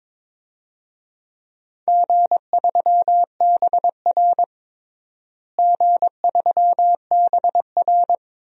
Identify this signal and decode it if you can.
Morse code, heard as a keyed tone, about 22 wpm, 700 Hz. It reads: G3BR G3BR
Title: DXCC-100 Call Signs - 4 Characters - Repeated 2x - 22wpm